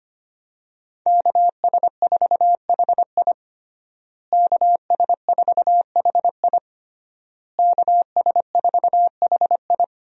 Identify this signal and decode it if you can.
Morse code, heard as a keyed tone, about 25 wpm, 700 Hz. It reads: KS4HI KS4HI KS4HI
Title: USA Call Signs - 5 Characters - Repeated 3x - 25wpm